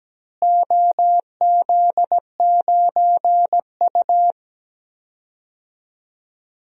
Morse code, heard as a keyed tone, about 17 wpm, 700 Hz. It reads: OZ9U
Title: DXCC-100 Call Signs - 4 Characters - Repeated 3x - 17wpm